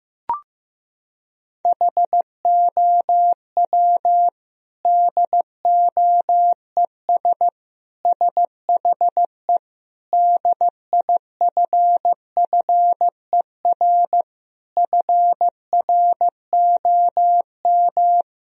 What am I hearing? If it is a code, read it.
Morse code, heard as a keyed tone, about 15 wpm, 700 Hz. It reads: HOW DOES SHE DIFFER FROM
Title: Sentences from Top 1000 Words - Review Part 1 - 15wpm